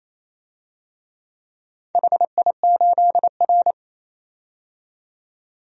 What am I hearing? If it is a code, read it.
Morse code, heard as a keyed tone, about 27 wpm, 700 Hz. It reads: HI8R
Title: DXCC-100 Call Signs - 4 Characters - 28wpm